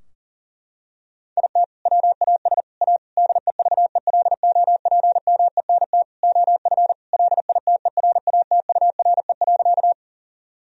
Morse code, heard as a keyed tone, about 40 wpm, 700 Hz. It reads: IT WAS A DEVELOPMENT OF LITERATURE.
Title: Sentences from Top 1000 Words - Review Part 4 - 40wpm